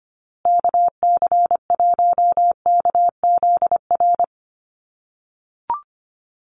Morse code, heard as a keyed tone, about 25 wpm, 700 Hz. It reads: KC1KZR
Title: USA Call Signs Repeated 2x 25wpm